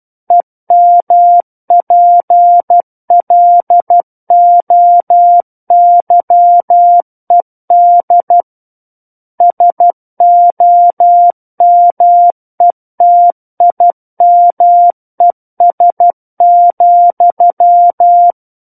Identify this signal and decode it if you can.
Morse code, heard as a keyed tone, about 12 wpm, 700 Hz. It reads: EMPLOYED SOMETIMES,
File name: Book One / 03-Chapter 3 - 12 WPM